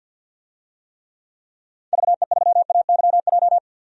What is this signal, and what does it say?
Morse code, heard as a keyed tone, about 47 wpm, 700 Hz. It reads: VE3AYJ
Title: DXCC-100 Call Signs - 6 Characters - 50wpm